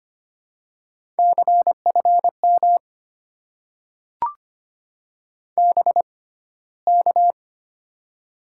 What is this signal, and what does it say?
Morse code, heard as a keyed tone, about 25 wpm, spaced as for 15 wpm, 700 Hz. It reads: CFM BK
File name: Top 90 QSO Elements - 6x Character Spacing - 25wpm